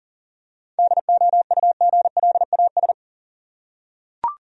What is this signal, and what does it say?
Morse code, heard as a keyed tone, about 40 wpm, 700 Hz. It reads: DOUGLAS
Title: USA Common Names - 40wpm